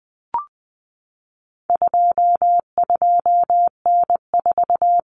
Morse code, heard as a keyed tone, about 20 wpm, 700 Hz. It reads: 22N4
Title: Common Cut Numbers - 4-Digit 20wpm